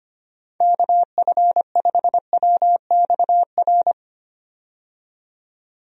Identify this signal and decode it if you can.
Morse code, heard as a keyed tone, about 25 wpm, 700 Hz. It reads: KF5WXR
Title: USA Call Signs - 6 Characters - 25wpm